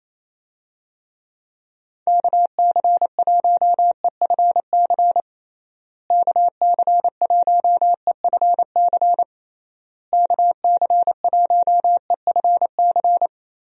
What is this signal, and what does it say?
Morse code, heard as a keyed tone, about 28 wpm, 700 Hz. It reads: KC1EFC KC1EFC KC1EFC
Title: USA Call Signs - 6 Characters - Repeated 3x - 28wpm